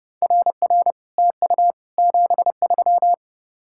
Morse code, heard as a keyed tone, about 30 wpm, 700 Hz. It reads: RR TU 73